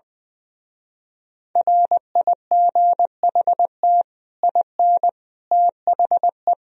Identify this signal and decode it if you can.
Morse code, heard as a keyed tone, about 20 wpm, 700 Hz. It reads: RIGHT IN THE
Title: Speed-Racing - Sets of 5 Words 20wpm